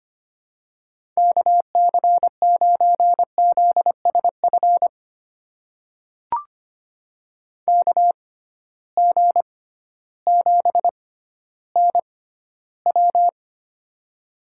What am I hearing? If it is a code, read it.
Morse code, heard as a keyed tone, about 25 wpm, spaced as for 13 wpm, 700 Hz. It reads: KC9ZSF KG7NW
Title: USA Call Signs - 6x Character Spacing - 25wpm